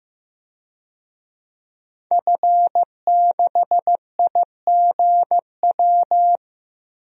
Morse code, heard as a keyed tone, about 15 wpm, 700 Hz. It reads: F6IGW
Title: DXCC-100 Call Signs - 5 Characters - 15wpm